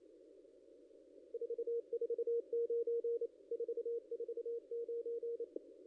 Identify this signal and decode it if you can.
Morse code, heard as a keyed tone, about 28 wpm, 440 Hz. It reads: 449 449
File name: contests / ss-cw-k5tr-op-n5rz -0200